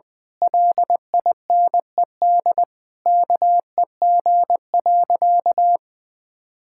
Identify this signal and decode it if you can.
Morse code, heard as a keyed tone, about 20 wpm, 700 Hz. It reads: LINED KEG.
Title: Pangram Challenge v2 - 20wpm